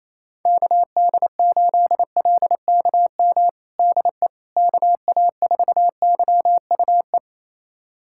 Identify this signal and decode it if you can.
Morse code, heard as a keyed tone, about 28 wpm, 700 Hz. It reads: KD8LKM DE KA4YUE
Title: USA Callsign DE Callsign - 6 Characters - 28wpm